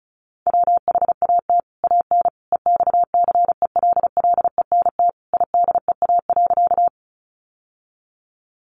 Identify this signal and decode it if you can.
Morse code, heard as a keyed tone, about 34 wpm, 700 Hz. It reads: WHAT AN EXCELLENT IDEA.
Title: Sentences from Top 1000 Words 35wpm